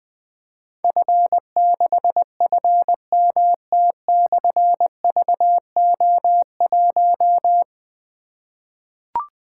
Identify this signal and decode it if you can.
Morse code, heard as a keyed tone, about 20 wpm, 700 Hz. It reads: F6FMT/VO1
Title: Difficult Call Signs 20wpm